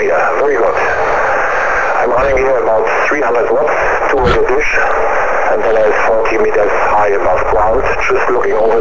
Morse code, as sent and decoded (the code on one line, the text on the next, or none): none